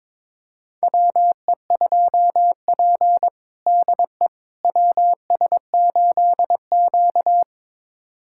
Morse code, heard as a keyed tone, about 21 wpm, 700 Hz. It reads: WE2P DE WS8Q
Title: USA Callsign DE Callsign - 4 Characters - 22wpm